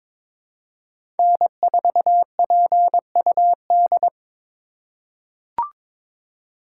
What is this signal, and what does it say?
Morse code, heard as a keyed tone, about 22 wpm, 700 Hz. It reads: N4PUD
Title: USA Call Signs - 5 Characters - 3x Character Spacing - 22wpm